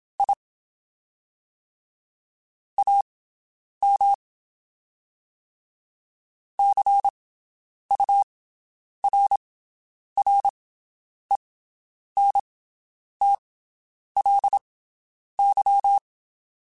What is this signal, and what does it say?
Morse code, heard as a keyed tone, about 27 wpm, spaced as for 9 wpm, 800 Hz. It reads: I AM CURRENTLY
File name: QSOs Spaced